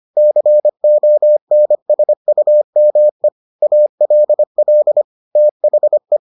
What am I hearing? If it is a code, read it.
Morse code, heard as a keyed tone, about 25 wpm, 600 Hz. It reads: CONSUME ALL THE